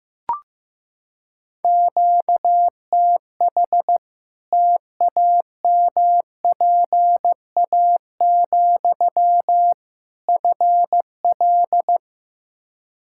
Morse code, heard as a keyed tone, about 15 wpm, 700 Hz. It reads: QTH TAMPA, FL
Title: Practice QSO - Exchange 1 - Beginning - 15wpm